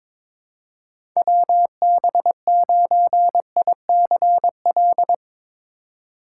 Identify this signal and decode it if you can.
Morse code, heard as a keyed tone, about 22 wpm, 700 Hz. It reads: WB9ICL
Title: USA Call Signs - 6 Characters - 3x Character Spacing - 22wpm